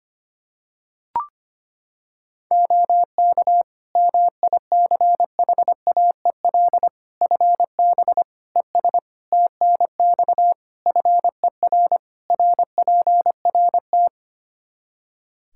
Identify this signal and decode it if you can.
Morse code, heard as a keyed tone, about 25 wpm, 700 Hz. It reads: OK MICHAEL FB ES TNX FER RPRT
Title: K1USN SST QSO Practice 25wpm